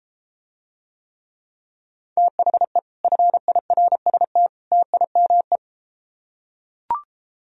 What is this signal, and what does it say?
Morse code, heard as a keyed tone, about 33 wpm, 700 Hz. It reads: THE FIRST TIME